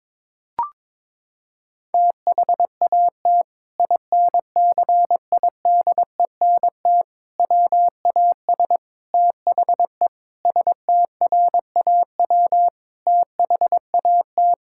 Morse code, heard as a keyed tone, about 21 wpm, 700 Hz. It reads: THAT INCIDENT WAS THE STRAW THAT
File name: practice / 22wpm / Sentences with 8-Word Phrases - 22wpm